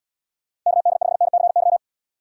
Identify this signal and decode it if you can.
Morse code, heard as a keyed tone, about 77 wpm, 700 Hz. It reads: KD5NCY